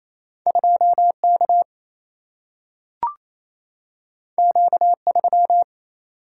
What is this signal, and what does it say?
Morse code, heard as a keyed tone, about 28 wpm, 700 Hz. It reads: 2K Q3
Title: Single-Letter Single-Number 28wpm